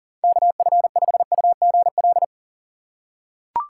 Morse code, heard as a keyed tone, about 40 wpm, 700 Hz. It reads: KF5UGL
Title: USA Call Signs - 4x Character Spacing - 40wpm